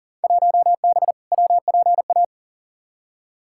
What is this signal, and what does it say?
Morse code, heard as a keyed tone, about 40 wpm, 700 Hz. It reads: 1B WPA
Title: Field Day Exchanges Encore - 40wpm